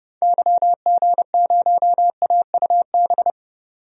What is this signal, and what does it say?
Morse code, heard as a keyed tone, about 30 wpm, 690 Hz. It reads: YG0AUB
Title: DXCC-100 Call Signs - 6 Characters - 30wpm